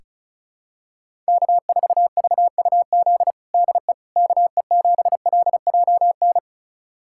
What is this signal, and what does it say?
Morse code, heard as a keyed tone, about 36 wpm, 700 Hz. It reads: K4VUZ DE KE7LJN
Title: USA Callsign DE Callsign 35wpm